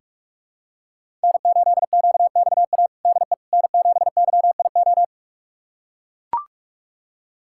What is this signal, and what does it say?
Morse code, heard as a keyed tone, about 42 wpm, 700 Hz. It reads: N8QXA DE N7YIO
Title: USA Callsign DE Callsign - 5 Characters - 45wpm